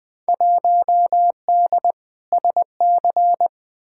Morse code, heard as a keyed tone, about 21 wpm, 700 Hz. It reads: 1D SC
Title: Field Day Exchanges - 20wpm